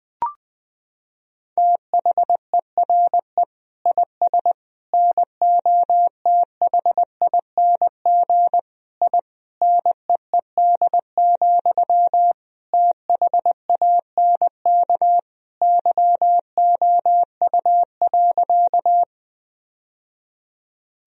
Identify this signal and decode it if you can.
Morse code, heard as a keyed tone, about 20 wpm, 700 Hz. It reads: THERE IS NOTHING I NEED, THANK YOU.
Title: Sentences from Top 1000 Words - Review Part 1 - 20wpm